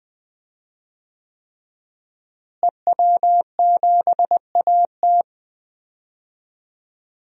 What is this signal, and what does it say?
Morse code, heard as a keyed tone, about 20 wpm, 700 Hz. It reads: EW7AT
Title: DXCC-100 Call Signs - 5 Characters - Repeated 3x - 20wpm